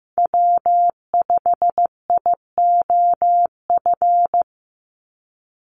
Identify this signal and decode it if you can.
Morse code, heard as a keyed tone, about 15 wpm, 700 Hz. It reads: W5IOF
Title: USA Call Signs 15wpm